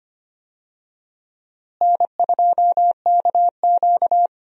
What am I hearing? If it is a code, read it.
Morse code, heard as a keyed tone, about 25 wpm, 700 Hz. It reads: N2KQ
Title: USA Call Signs - 4 Characters - Repeated 2x - 25wpm